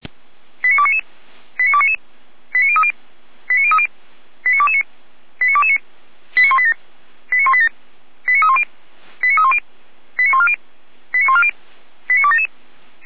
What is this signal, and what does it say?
Five-tone selective call (ZVEI), heard as quick runs of five tones, tones of about 70 ms. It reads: 89299 89299 89030 89030 89229 89229 89187 89187 89310 89310 89140 89140 89277